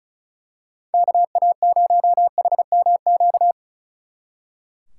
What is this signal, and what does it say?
Morse code, heard as a keyed tone, about 33 wpm, 700 Hz. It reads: KA0HMQ
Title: Speed Racing - USA 2x3 Call Signs 35wpm